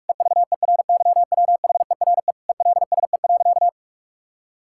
Morse code, heard as a keyed tone, about 42 wpm, 700 Hz. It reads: EVERYWHERE ELSE.
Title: Sentences from Top 1600 Words 45wpm